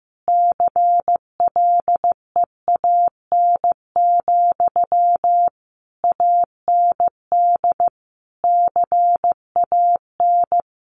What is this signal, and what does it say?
Morse code, heard as a keyed tone, about 15 wpm, 700 Hz. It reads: CLEAN, AND CAN
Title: Sentences from Top 1100 Words 15wpm